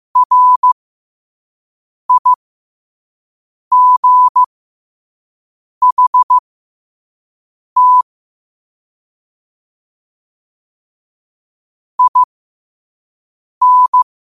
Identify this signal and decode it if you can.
Morse code, heard as a keyed tone, about 15 wpm, spaced as for 5 wpm, 1000 Hz. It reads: RIGHT IN
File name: qotdfiles / 5wpm